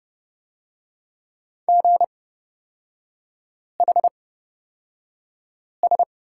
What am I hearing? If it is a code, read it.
Morse code, heard as a keyed tone, about 30 wpm, spaced as for 5 wpm, 700 Hz. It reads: GHS
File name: Lesson 38e - Letter q - Three Characters 30wpm